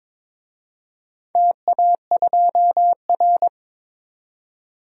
Morse code, heard as a keyed tone, about 22 wpm, 700 Hz. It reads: TA2R
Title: DXCC-100 Call Signs - 4 Characters - 22wpm